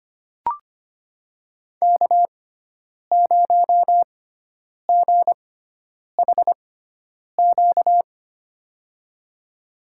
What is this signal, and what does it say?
Morse code, heard as a keyed tone, about 25 wpm, spaced as for 9 wpm, 700 Hz. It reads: K0GHQ
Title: USA Call Signs - 5 Characters - 6x Character Spacing - 25wpm